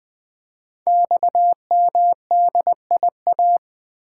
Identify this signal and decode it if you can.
Morse code, heard as a keyed tone, about 20 wpm, 700 Hz. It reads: XMDIA
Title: Code Groups - 5 Letters - 20wpm